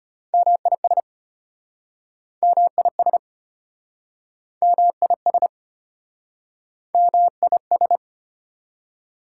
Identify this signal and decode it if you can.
Morse code, heard as a keyed tone, about 35 wpm, 700 Hz. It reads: MIS MIS MIS MIS